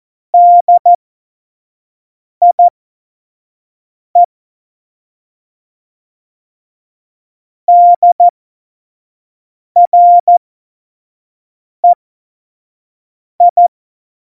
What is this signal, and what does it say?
Morse code, heard as a keyed tone, about 14 wpm, spaced as for 5 wpm, 700 Hz. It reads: DIE DREI
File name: Morsekurs / Witze / Witz 23. April 2019, 25 F70